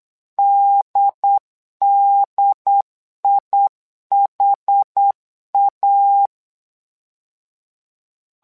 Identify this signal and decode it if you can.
Morse code, heard as a keyed tone, about 8 wpm, 800 Hz. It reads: DDIHA
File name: LEZIONI / 2 UHDN / 2 UHD 8wpm 2258